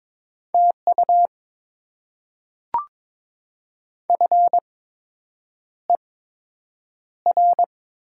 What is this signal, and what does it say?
Morse code, heard as a keyed tone, about 22 wpm, spaced as for 8 wpm, 700 Hz. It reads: TU FER
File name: Top 20 QSO Elements - 8x Character Spacing - 22wpm